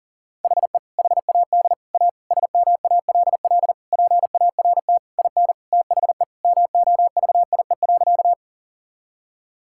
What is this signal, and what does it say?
Morse code, heard as a keyed tone, about 40 wpm, 700 Hz. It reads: HE HAD A SMALL PART IN THE MOVIE.
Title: Sentences from Top 1000 Words - Review Part 5 - 40wpm